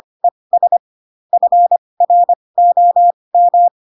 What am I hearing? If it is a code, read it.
Morse code, heard as a keyed tone, about 25 wpm, spaced as for 20 wpm, 700 Hz. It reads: ES FROM
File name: Pinocchio / Pinocchio ch8 15 25 ik1hge